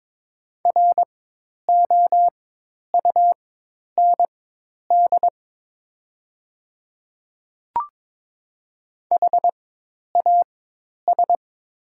Morse code, heard as a keyed tone, about 22 wpm, spaced as for 10 wpm, 700 Hz. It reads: ROUND HAS
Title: 4x Character Spacing - Top 300 Words - 22wpm